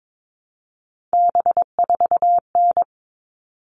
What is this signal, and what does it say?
Morse code, heard as a keyed tone, about 22 wpm, 700 Hz. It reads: B4N
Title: All Cut Numbers - 3-Digit 22wpm